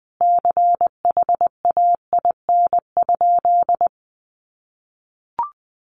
Morse code, heard as a keyed tone, about 20 wpm, 700 Hz. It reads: CHAIN?
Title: Sentences from Top 1500 Words 20wpm